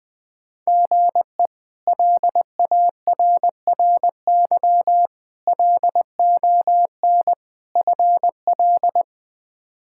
Morse code, heard as a keyed tone, about 20 wpm, 700 Hz. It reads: GE LARRY LON FL
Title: K1USN SST QSO Practice - 20wpm